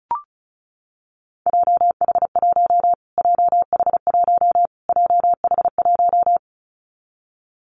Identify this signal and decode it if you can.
Morse code, heard as a keyed tone, about 34 wpm, 700 Hz. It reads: JH1 JH1 JH1
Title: Callsign Prefixes for Top 25 DXCC - Repeated 3x - 35wpm